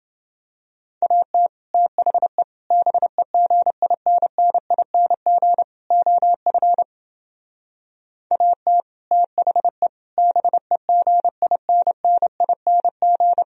Morse code, heard as a keyed tone, about 29 wpm, 700 Hz. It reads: AT THE BEGINNING OF AT THE BEGINNING